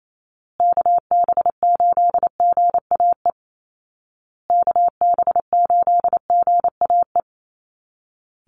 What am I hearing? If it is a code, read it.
Morse code, heard as a keyed tone, about 28 wpm, 700 Hz. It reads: KB8GAE KB8GAE
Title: Call Signs Repeated 2x - 28wpm